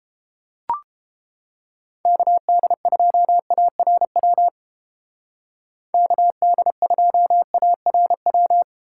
Morse code, heard as a keyed tone, about 33 wpm, 700 Hz. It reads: KD2ARW KD2ARW